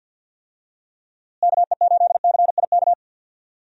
Morse code, heard as a keyed tone, about 52 wpm, 700 Hz. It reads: KE8CIK
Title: USA Call Signs - 6 Characters - Repeated 3x - 50wpm